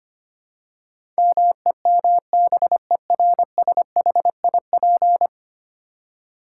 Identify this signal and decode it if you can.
Morse code, heard as a keyed tone, about 25 wpm, 700 Hz. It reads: MEMBERSHIP
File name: Ten Letter Words - 25wpm